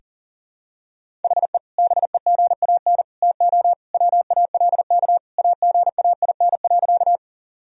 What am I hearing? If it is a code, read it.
Morse code, heard as a keyed tone, about 37 wpm, 700 Hz. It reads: HE BEGAN TO WALK AGAIN.
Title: Sentences from Top 1000 Words - Review Part 1 - 40wpm